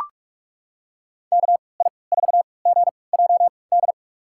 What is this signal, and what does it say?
Morse code, heard as a keyed tone, about 41 wpm, spaced as for 25 wpm, 700 Hz. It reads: KI4GJD
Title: USA Call Signs - 6 Characters - 3x Character Spacing - 45wpm